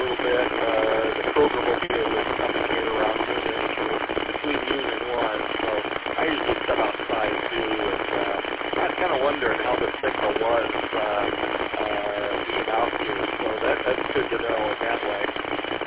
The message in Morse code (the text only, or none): none